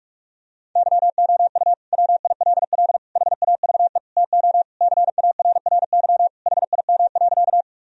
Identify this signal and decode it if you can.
Morse code, heard as a keyed tone, about 46 wpm, 700 Hz. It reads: YOU WILL HAVE TO CARRY HIM.
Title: Sentences from Top 1000 Words - Review Part 1 - 45wpm